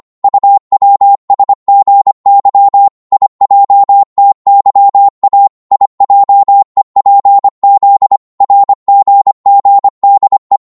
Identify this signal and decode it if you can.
Morse code, heard as a keyed tone, about 25 wpm, 800 Hz. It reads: UWSGY IJTYA IJEPZ RGGDE
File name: Morse / ltr-25wpm